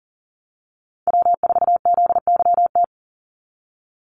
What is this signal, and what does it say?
Morse code, heard as a keyed tone, about 40 wpm, 700 Hz. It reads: W4ZYT
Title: Call Signs - 40wpm